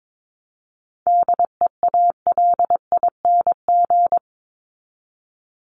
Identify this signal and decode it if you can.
Morse code, heard as a keyed tone, about 22 wpm, 700 Hz. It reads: DEALING
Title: ING Suffix - Top 100 Words with Base Words 22wpm